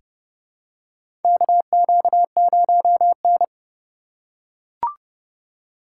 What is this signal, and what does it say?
Morse code, heard as a keyed tone, about 30 wpm, 700 Hz. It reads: KQ0N
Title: USA Call Signs - 4 Characters - 3x Character Spacing - 30wpm